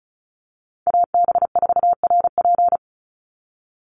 Morse code, heard as a keyed tone, about 34 wpm, 700 Hz. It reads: AB4RP
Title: USA Call Signs 35wpm